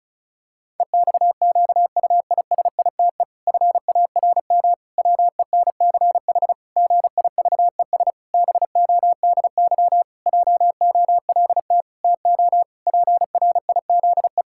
Morse code, heard as a keyed tone, about 36 wpm, 700 Hz. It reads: EXQUISITE FARM WENCH GIVES BODY JOLT TO PRIZE
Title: Pangram Challenge v2 - 35wpm